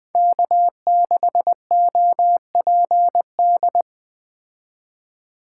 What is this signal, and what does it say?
Morse code, heard as a keyed tone, about 20 wpm, 700 Hz. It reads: K6OPD
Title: Speed Racing - USA 1x3 Call Signs 17wpm